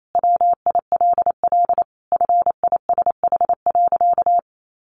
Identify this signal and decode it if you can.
Morse code, heard as a keyed tone, about 28 wpm, 700 Hz. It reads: WILL FISH.
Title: Sentences from Top 1100 Words 28wpm